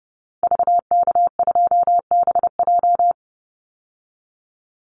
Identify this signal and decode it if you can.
Morse code, heard as a keyed tone, about 30 wpm, 700 Hz. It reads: VK2BJ
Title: Call Signs Repeated 2x - 30wpm